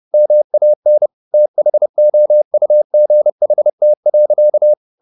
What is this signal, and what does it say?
Morse code, heard as a keyed tone, about 30 wpm, 600 Hz. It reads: MAN THOUGHT.
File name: the old man and the sea / the old man and the sea 1 30wpm